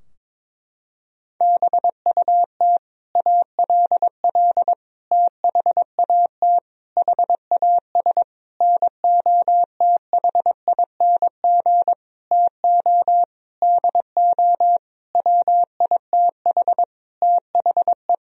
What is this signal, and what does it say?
Morse code, heard as a keyed tone, about 22 wpm, 700 Hz. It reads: BUT ALL THAT HAS NOTHING TO DO WITH THE
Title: Sentences from Top 1000 Words - Review Part 3 - 22wpm